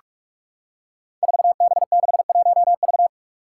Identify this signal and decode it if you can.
Morse code, heard as a keyed tone, about 44 wpm, 700 Hz. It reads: 4B61V
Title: Five Challenging Letter-Number Combinations 45wpm